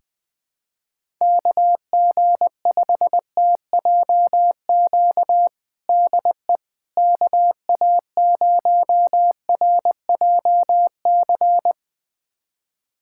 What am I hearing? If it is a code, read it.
Morse code, heard as a keyed tone, about 21 wpm, 700 Hz. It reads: KG5TJQ DE KA0RJC
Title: USA Callsign DE Callsign - 6 Characters - 20wpm